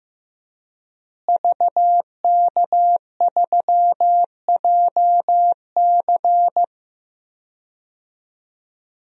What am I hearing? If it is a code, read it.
Morse code, heard as a keyed tone, about 15 wpm, 700 Hz. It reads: VK3JC
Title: DXCC-100 Call Signs - 5 Characters - Repeated 2x - 15wpm